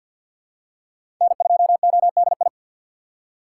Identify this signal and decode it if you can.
Morse code, heard as a keyed tone, about 47 wpm, 700 Hz. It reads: N2ODI
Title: USA Call Signs - 5 Characters - 2x Character Spacing - 50wpm